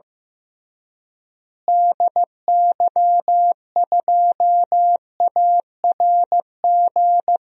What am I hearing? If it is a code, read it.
Morse code, heard as a keyed tone, about 15 wpm, 700 Hz. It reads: DY2ARG
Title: DXCC-100 Call Signs - 6 Characters - Repeated 3x - 15wpm